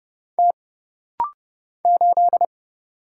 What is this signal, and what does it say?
Morse code, heard as a keyed tone, about 29 wpm, 700 Hz. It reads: T 8
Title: Single Letter-Number Alt - Rapid-Fire 30wpm